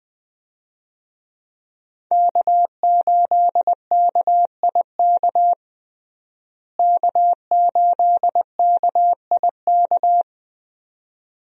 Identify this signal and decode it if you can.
Morse code, heard as a keyed tone, about 20 wpm, 700 Hz. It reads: K8KIK K8KIK
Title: USA Call Signs - 5 Characters - Repeated 2x - 20wpm